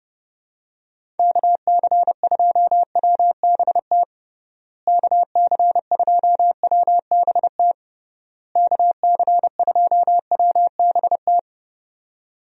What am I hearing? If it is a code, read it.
Morse code, heard as a keyed tone, about 30 wpm, 700 Hz. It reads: KC2WBT KC2WBT KC2WBT
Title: USA Call Signs - 6 Characters - Repeated 3x - 30wpm